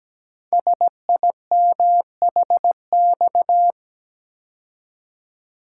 Morse code, heard as a keyed tone, about 17 wpm, 700 Hz. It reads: SIMHX
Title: Code Groups - 5 Letters - 17wpm